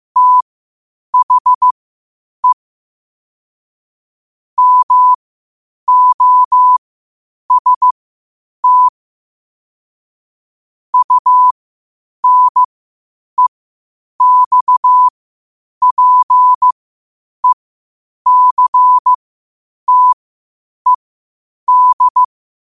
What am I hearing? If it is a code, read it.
Morse code, heard as a keyed tone, about 15 wpm, spaced as for 8 wpm, 1000 Hz. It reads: THE MOST UNEXPECTED
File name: qotdfiles / 8wpm